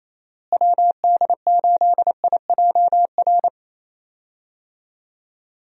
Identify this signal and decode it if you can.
Morse code, heard as a keyed tone, about 28 wpm, 700 Hz. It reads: WD8IJR
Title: USA Call Signs - 6 Characters - Repeated 3x - 28wpm